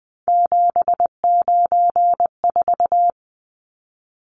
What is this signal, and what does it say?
Morse code, heard as a keyed tone, about 19 wpm, 700 Hz. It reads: 794
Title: Numbers - 3-Digit 20wpm